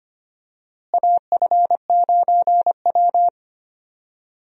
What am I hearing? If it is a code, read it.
Morse code, heard as a keyed tone, about 25 wpm, 700 Hz. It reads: AF9W
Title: USA Call Signs - 4 Characters - Repeated 3x - 25wpm